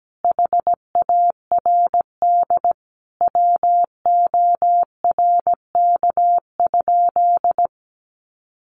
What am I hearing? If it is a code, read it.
Morse code, heard as a keyed tone, about 17 wpm, 700 Hz. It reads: HARD WORK?
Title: Sentences from Top 1400 Words 17wpm